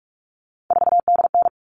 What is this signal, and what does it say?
Morse code, heard as a keyed tone, about 44 wpm, 700 Hz. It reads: 4DN
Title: All Cut Numbers - 3-Digit 45wpm